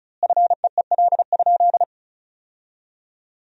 Morse code, heard as a keyed tone, about 36 wpm, 700 Hz. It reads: FEEL?
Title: 4x Word-Spacing - Sentences from Top 400 Words - 35wpm